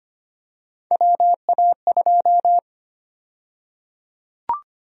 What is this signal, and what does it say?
Morse code, heard as a keyed tone, about 25 wpm, 700 Hz. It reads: WA2
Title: USA Call Sign Prefixes - Repeated 2x - 25wpm